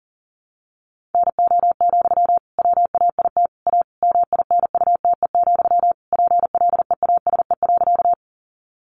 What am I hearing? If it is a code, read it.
Morse code, heard as a keyed tone, about 40 wpm, 700 Hz. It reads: NO, WAIT A MINUTE, PLEASE.
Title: Sentences from Top 1500 Words 40wpm